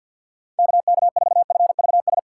Morse code, heard as a keyed tone, about 47 wpm, 700 Hz. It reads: KK4FVS
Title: USA Call Signs - 6 Characters - 6x Character Spacing - 50wpm